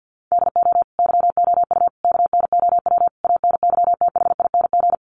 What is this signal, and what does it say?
Morse code, heard as a keyed tone, about 50 wpm, 700 Hz. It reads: DO YOU KNOW ANYTHING
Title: Sentences from Top 1000 Words 50wpm